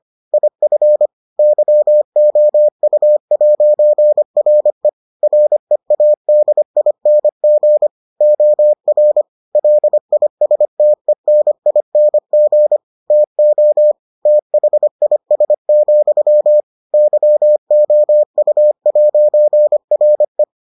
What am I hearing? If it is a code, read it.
Morse code, heard as a keyed tone, about 25 wpm, 600 Hz. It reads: IF YOU'RE READING OR LISTENING TO THIS, YOU'RE